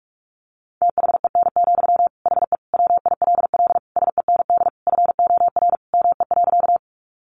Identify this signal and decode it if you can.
Morse code, heard as a keyed tone, about 44 wpm, 700 Hz. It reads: THEN, HE WILL SEND FOR ME.
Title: Sentences from Top 1500 Words 45wpm